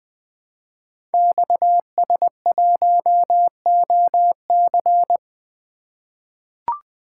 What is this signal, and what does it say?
Morse code, heard as a keyed tone, about 20 wpm, 700 Hz. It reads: XS1OC
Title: Code Groups - 5 Letter:Numbers - 20wpm